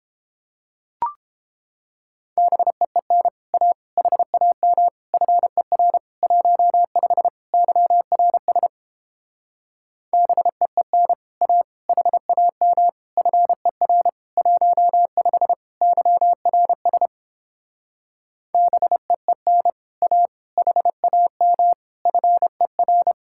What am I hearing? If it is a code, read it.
Morse code, heard as a keyed tone, about 29 wpm, 700 Hz. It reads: BEEN A HAM FER 15 YRS BEEN A HAM FER 15 YRS BEEN A HAM FER